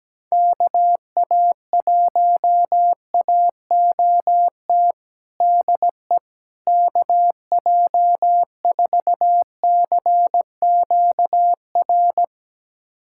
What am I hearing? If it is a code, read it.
Morse code, heard as a keyed tone, about 17 wpm, 700 Hz. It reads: KA1AOT DE KJ4CQR
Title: USA Callsign DE Callsign 17wpm